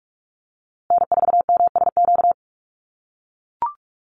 Morse code, heard as a keyed tone, about 44 wpm, 690 Hz. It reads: N4MSQ
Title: USA Call Signs Repeated 2x 45wpm